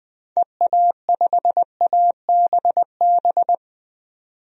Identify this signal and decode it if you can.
Morse code, heard as a keyed tone, about 20 wpm, 700 Hz. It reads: EA5ABB
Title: DXCC-100 Call Signs - 6 Characters - 20wpm